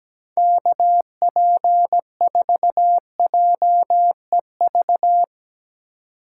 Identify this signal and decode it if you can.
Morse code, heard as a keyed tone, about 19 wpm, 700 Hz. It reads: KP4JEV
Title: Speed Racing - USA 2x3 Call Signs 17wpm